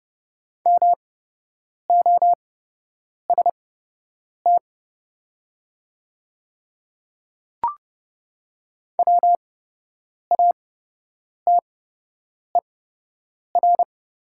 Morse code, heard as a keyed tone, about 30 wpm, spaced as for 8 wpm, 700 Hz. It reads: MOST WATER
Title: 8x Character Spacing - Top 300 Words - 30wpm